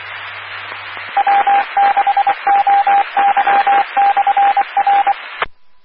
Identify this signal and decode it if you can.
Morse code, heard as a keyed tone, about 24 wpm, 800 Hz, with radio noise. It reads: W6OY/R
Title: Claremont, CA Police, Amateur Radio